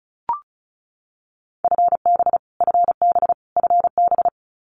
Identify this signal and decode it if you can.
Morse code, heard as a keyed tone, about 34 wpm, 700 Hz. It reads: FB FB FB
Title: Abbreviations Repeated 3x - 35wpm